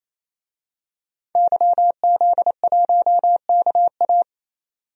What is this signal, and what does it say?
Morse code, heard as a keyed tone, about 28 wpm, 700 Hz. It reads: YZ1KA
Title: DXCC-100 Call Signs - 5 Characters - 28wpm